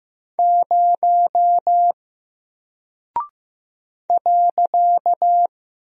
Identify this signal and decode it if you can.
Morse code, heard as a keyed tone, about 15 wpm, 700 Hz. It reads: 0 .
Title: Single Character (Letter:Number:Punctuation:Prosign) Alt 15wpm